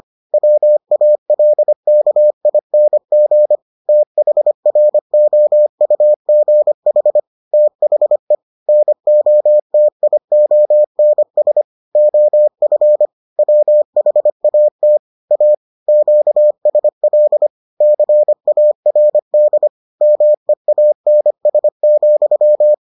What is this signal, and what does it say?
Morse code, heard as a keyed tone, about 25 wpm, 600 Hz. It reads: WALKING THROUGH THE NOTIONS OF WHAT A QSL CARD MEANS,